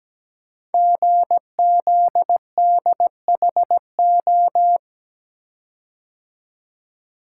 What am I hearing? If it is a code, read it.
Morse code, heard as a keyed tone, about 17 wpm, 700 Hz. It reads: GZDHO
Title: Code Groups - 5 Letters - 17wpm